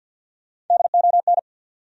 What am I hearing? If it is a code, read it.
Morse code, heard as a keyed tone, about 53 wpm, 700 Hz. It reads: DON
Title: USA Common Names - Max 4 Letters - 50wpm